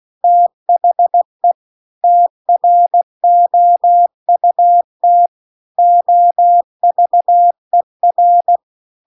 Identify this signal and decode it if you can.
Morse code, heard as a keyed tone, about 16 wpm, 700 Hz. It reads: THE TROUT OVER